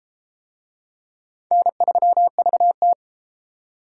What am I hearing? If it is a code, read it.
Morse code, heard as a keyed tone, about 29 wpm, 700 Hz. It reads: N3VT